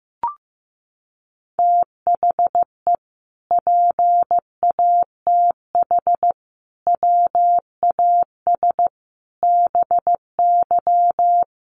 Morse code, heard as a keyed tone, about 14 wpm, 700 Hz. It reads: THE PATH WAS BY
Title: Sentences from Top 1400 Words 15wpm